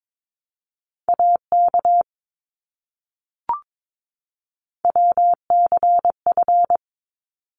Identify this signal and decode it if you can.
Morse code, heard as a keyed tone, about 22 wpm, 700 Hz. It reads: AK WCF
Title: ARRL Sections 22wpm